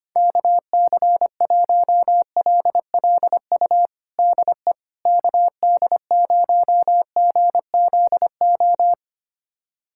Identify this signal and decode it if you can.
Morse code, heard as a keyed tone, about 25 wpm, 700 Hz. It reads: KC1LLU DE KD0GZO
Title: USA Callsign DE Callsign 25wpm